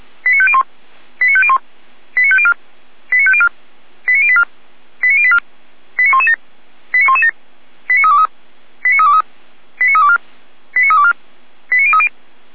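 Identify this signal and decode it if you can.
Five-tone selective call (ZVEI), heard as quick runs of five tones, tones of about 70 ms. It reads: 89501 89501 89504 89504 89084 89084 89117 89117 89323 89323 89324 89324 89030
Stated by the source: ZVEI five-tone call 89125 received by SDR.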